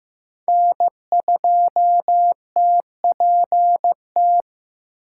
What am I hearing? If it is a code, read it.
Morse code, heard as a keyed tone, about 15 wpm, 700 Hz. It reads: N2TPT
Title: USA Call Signs - 4x Character Spacing - 15wpm